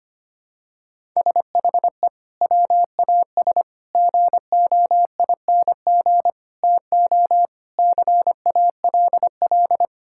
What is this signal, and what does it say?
Morse code, heard as a keyed tone, about 25 wpm, 700 Hz. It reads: SHE WAS GOING TO CALL
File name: practice / 25wpm / Sentences with 4-Word Phrases - 25wpm